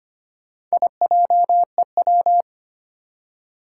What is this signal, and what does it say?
Morse code, heard as a keyed tone, about 25 wpm, 700 Hz. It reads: IJEW
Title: Four Challenging Letter-Number Combinations 25wpm